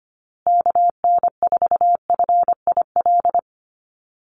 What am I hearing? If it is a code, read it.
Morse code, heard as a keyed tone, about 25 wpm, 700 Hz. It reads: KN4FIL
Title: USA Call Signs Repeated 3x 25wpm